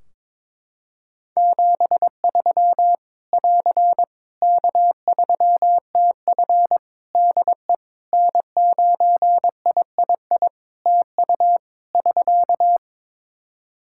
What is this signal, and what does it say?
Morse code, heard as a keyed tone, about 22 wpm, 700 Hz. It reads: 73 <AR> K3TF DE N9III TU <SK>
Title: Practice QSO - Exchange 4 - Ending - 22wpm